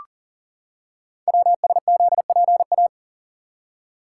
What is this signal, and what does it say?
Morse code, heard as a keyed tone, about 38 wpm, 700 Hz. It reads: WSZPA
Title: Code Groups - 5 Letters - 40wpm